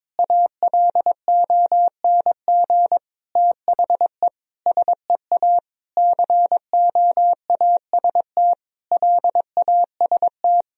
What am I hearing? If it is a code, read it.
Morse code, heard as a keyed tone, about 22 wpm, 700 Hz. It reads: ALONG THE SEA COAST LAST
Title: Sentences from Top 1000 Words - Review Part 4 - 22wpm